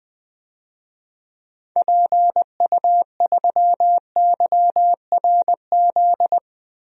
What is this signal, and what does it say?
Morse code, heard as a keyed tone, about 20 wpm, 700 Hz. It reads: PU3YRZ
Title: DXCC-100 Call Signs - 6 Characters - Repeated 3x - 20wpm